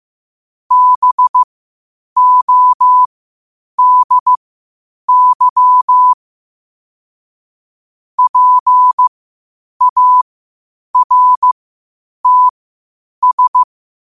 Morse code, heard as a keyed tone, about 15 wpm, spaced as for 8 wpm, 1000 Hz. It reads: BODY PARTS
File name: qotdfiles / 8wpm